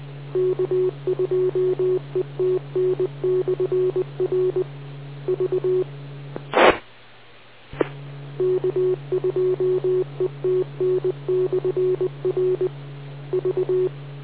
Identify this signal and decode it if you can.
Morse code, heard as a keyed tone, about 20 wpm, 370 Hz. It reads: K2ETN/R V T E K2ETN/R V